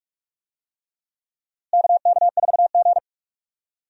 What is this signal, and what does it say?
Morse code, heard as a keyed tone, about 46 wpm, 700 Hz. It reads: KK4G
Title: USA Call Signs - 4 Characters - Repeated 3x - 45wpm